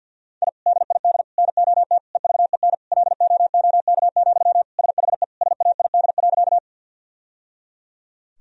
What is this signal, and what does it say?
Morse code, heard as a keyed tone, about 47 wpm, 700 Hz. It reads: I DID NOT EVEN LOOK, SHE SAID.
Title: Sentences from Top 1000 Words - Review Part 1 - 50wpm